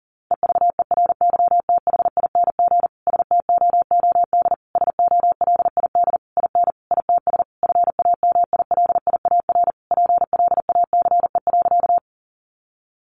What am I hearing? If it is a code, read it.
Morse code, heard as a keyed tone, about 40 wpm, 700 Hz. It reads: EVERYTHING STOOD SOLID IN ITS FAMILIAR PLACE.
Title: Sentences from Top 1500 Words 40wpm